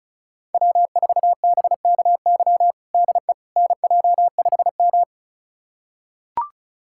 Morse code, heard as a keyed tone, about 36 wpm, 700 Hz. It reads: W4BKY DE NJ5M
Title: USA Callsign DE Callsign 35wpm